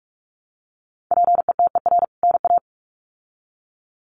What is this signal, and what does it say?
Morse code, heard as a keyed tone, about 44 wpm, 700 Hz. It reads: PETER NA
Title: CWT Exchange 45wpm